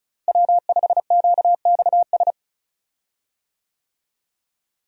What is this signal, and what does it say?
Morse code, heard as a keyed tone, about 34 wpm, 700 Hz. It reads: W5QXS
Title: USA Call Signs - 5 Characters - Repeated 3x - 35wpm